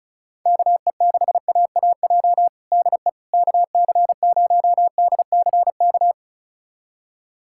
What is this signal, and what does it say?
Morse code, heard as a keyed tone, about 36 wpm, 700 Hz. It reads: KE6AAJ DE KC0DCK
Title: USA Callsign DE Callsign - 6 Characters - 35wpm